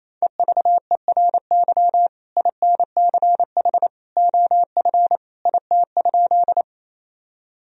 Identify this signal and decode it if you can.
Morse code, heard as a keyed tone, about 28 wpm, 700 Hz. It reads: EVERY INCH OF IT?
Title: Sentences from Top 1000 Words - Review Part 3 - 28wpm